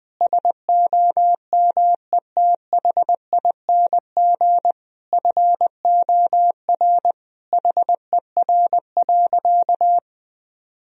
Morse code, heard as a keyed tone, about 21 wpm, 700 Hz. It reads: SOMETHING FOR HER.
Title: Sentences from Top 1000 Words - Review Part 1 - 20wpm